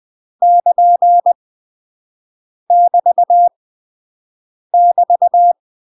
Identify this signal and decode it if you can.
Morse code, heard as a keyed tone, about 20 wpm, 700 Hz. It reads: <KN> <BT> <BT>
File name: UK MASTER QSOs TRAINING / 20WPM / UK MASTER QSOs 0037